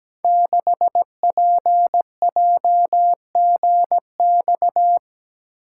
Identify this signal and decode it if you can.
Morse code, heard as a keyed tone, about 17 wpm, 700 Hz. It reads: 6PJGX
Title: Code Groups - 5 Letter:Numbers - 17wpm